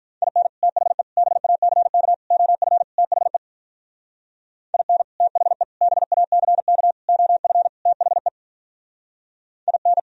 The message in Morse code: IN THE BACK OF THE IN THE BACK OF THE IN

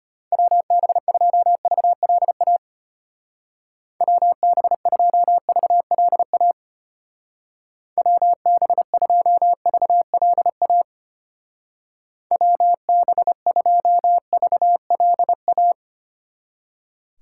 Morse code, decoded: WB2VLA WB2VLA WB2VLA WB2VLA

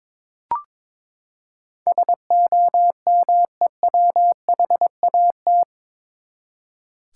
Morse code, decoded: SOMEWHAT

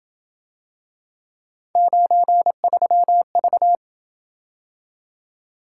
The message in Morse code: 93V

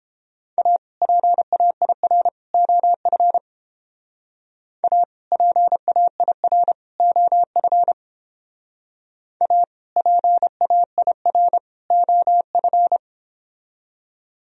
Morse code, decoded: A PAIR OF A PAIR OF A PAIR OF

